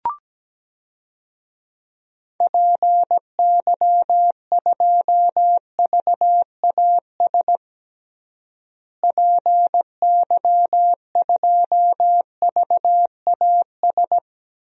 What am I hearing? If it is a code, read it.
Morse code, heard as a keyed tone, about 17 wpm, 700 Hz. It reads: PY2VAS PY2VAS
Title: DXCC-100 Call Signs - 6 Characters - Repeated 2x - 17wpm